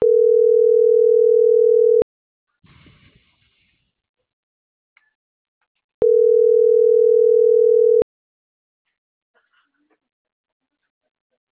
Ringback tone (North America), 440 + 480 Hz.